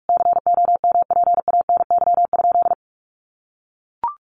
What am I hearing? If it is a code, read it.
Morse code, heard as a keyed tone, about 44 wpm, 700 Hz. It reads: COMPANY?
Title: Sentences from Top 600 Words 45wpm